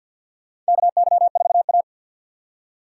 Morse code, heard as a keyed tone, about 51 wpm, 700 Hz. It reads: KY4A
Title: USA Call Signs - 3x Character Spacing - 50wpm